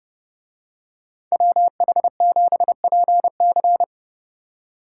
Morse code, heard as a keyed tone, about 29 wpm, 700 Hz. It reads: WH7PC